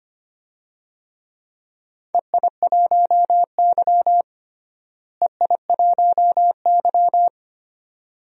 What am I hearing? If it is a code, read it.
Morse code, heard as a keyed tone, about 25 wpm, 700 Hz. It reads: EI1Y EI1Y
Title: DXCC-100 Call Signs - 4 Characters - Repeated 2x - 25wpm